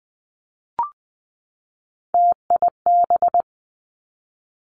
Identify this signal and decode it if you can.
Morse code, heard as a keyed tone, about 20 wpm, 700 Hz. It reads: TIB
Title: Three Challenging Letter-Number Combinations 20wpm